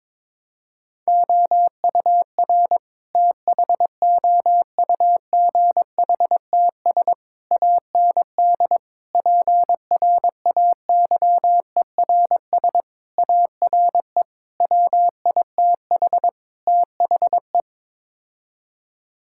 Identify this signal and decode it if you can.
Morse code, heard as a keyed tone, about 22 wpm, 700 Hz. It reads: OUR THOUGHTS AND PRAYERS ARE WITH THE
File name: practice / 22wpm / Sets of 7 Words - 22wpm